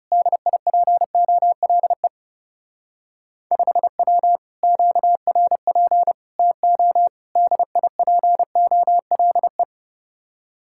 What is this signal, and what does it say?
Morse code, heard as a keyed tone, about 38 wpm, 700 Hz. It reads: DIPOLE 5W QRP TO DIPOLE